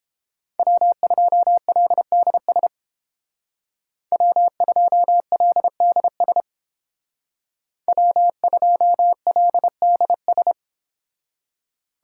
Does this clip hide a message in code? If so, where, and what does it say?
Yes: Morse code from 0.6 to 10.5 s, heard as a keyed tone, about 31 wpm, 700 Hz. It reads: W2LDS W2LDS W2LDS